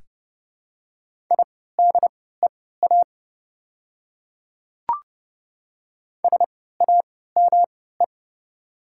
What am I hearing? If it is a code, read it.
Morse code, heard as a keyed tone, about 29 wpm, spaced as for 17 wpm, 690 Hz. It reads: IDEA SAME
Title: 3x Character Spacing - Top 300 Words - 30wpm